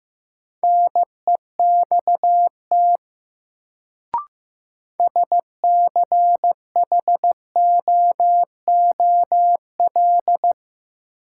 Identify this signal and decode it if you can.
Morse code, heard as a keyed tone, about 15 wpm, 700 Hz. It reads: NEXT SCHOOL
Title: Sentences from Top 300 Words - New Word Drill - 15wpm